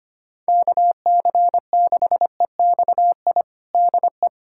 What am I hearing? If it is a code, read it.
Morse code, heard as a keyed tone, about 25 wpm, 700 Hz. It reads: KC6EXI DE
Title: USA Callsign DE Callsign 25wpm